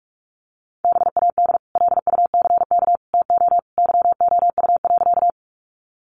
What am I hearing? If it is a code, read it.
Morse code, heard as a keyed tone, about 44 wpm, 700 Hz. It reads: BAD LUCK TO YOU.
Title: Sentences from Top 1500 Words 45wpm